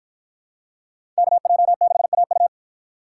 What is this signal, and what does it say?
Morse code, heard as a keyed tone, about 52 wpm, 700 Hz. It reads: KJ6AU